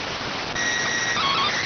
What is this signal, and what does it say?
SSTV (slow-scan television), PD 290.